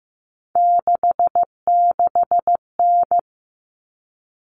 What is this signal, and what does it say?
Morse code, heard as a keyed tone, about 15 wpm, 700 Hz. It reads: 66N
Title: All Cut Numbers - 3-Digit 15wpm